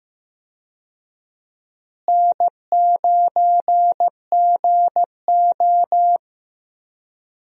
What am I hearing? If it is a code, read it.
Morse code, heard as a keyed tone, about 15 wpm, 700 Hz. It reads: N9GO
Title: USA Call Signs - 4 Characters - Repeated 3x - 15wpm